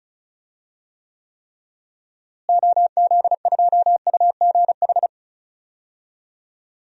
Morse code, heard as a keyed tone, about 36 wpm, 700 Hz. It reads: OZ2UGH
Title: DXCC-100 Call Signs - 6 Characters - Repeated 3x - 35wpm